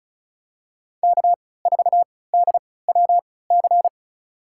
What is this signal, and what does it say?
Morse code, heard as a keyed tone, about 34 wpm, spaced as for 20 wpm, 700 Hz. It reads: K4DWC
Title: USA Call Signs - 5 Characters - 3x Character Spacing - 35wpm